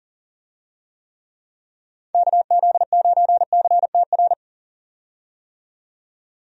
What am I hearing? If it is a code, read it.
Morse code, heard as a keyed tone, about 40 wpm, 700 Hz. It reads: KZ9CTR
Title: USA Call Signs - 6 Characters - 40wpm